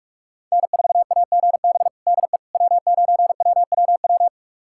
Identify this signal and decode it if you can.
Morse code, heard as a keyed tone, about 45 wpm, 700 Hz. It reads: N4AGB DE W9WWW
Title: USA Callsign DE Callsign - 5 Characters - 45wpm